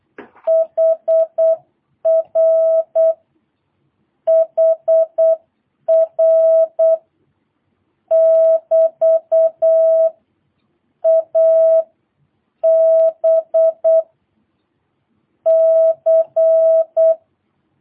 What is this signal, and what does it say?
Morse code, heard as a keyed tone, about 8 wpm, 650 Hz. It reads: HR HR <BT> A B C
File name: jh8cbh / 2022cw